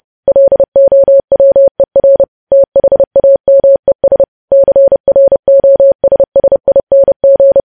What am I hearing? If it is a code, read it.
Morse code, heard as a keyed tone, about 30 wpm, 550 Hz. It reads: LOWER THAMES CROSSING
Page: news headlines in morse code at 30 wpm